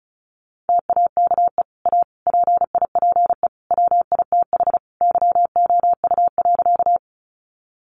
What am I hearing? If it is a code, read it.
Morse code, heard as a keyed tone, about 34 wpm, 700 Hz. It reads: TAKE A PIPE WITH YOU.
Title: Sentences from Top 1500 Words 35wpm